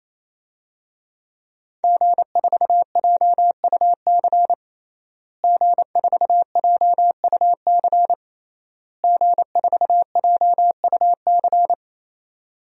Morse code, heard as a keyed tone, about 27 wpm, 700 Hz. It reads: G4JUC G4JUC G4JUC
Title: DXCC-100 Call Signs - 5 Characters - Repeated 3x - 28wpm